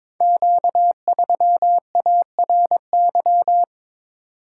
Morse code, heard as a keyed tone, about 22 wpm, 700 Hz. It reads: Q3ARY
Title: Code Groups - 5 Letter:Numbers - 22wpm